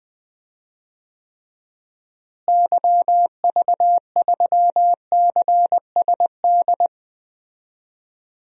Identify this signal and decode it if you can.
Morse code, heard as a keyed tone, about 21 wpm, 700 Hz. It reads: YV3CSD
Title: DXCC-100 Call Signs - 6 Characters - Repeated 2x - 20wpm